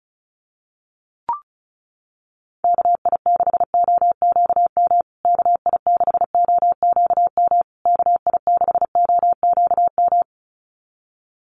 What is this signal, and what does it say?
Morse code, heard as a keyed tone, about 34 wpm, 700 Hz. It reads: KI6OQM KI6OQM KI6OQM
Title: USA Call Signs Repeated 3x 35wpm